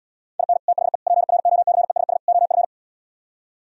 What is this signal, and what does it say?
Morse code, heard as a keyed tone, about 77 wpm, 700 Hz. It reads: IN THE PROCESS OF